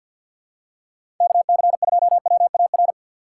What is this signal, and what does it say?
Morse code, heard as a keyed tone, about 52 wpm, 700 Hz. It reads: KC2WAR
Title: USA Call Signs - 4x Character Spacing - 50wpm